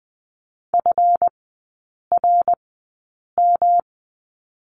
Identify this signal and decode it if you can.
Morse code, heard as a keyed tone, about 20 wpm, spaced as for 8 wpm, 700 Hz. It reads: FRM
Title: 218 Lesson 35e - Number 0 - Three Characters 20wpm